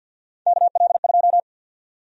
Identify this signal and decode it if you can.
Morse code, heard as a keyed tone, about 50 wpm, 700 Hz. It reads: KL2
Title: USA Call Sign Prefixes - Repeated 2x - 50wpm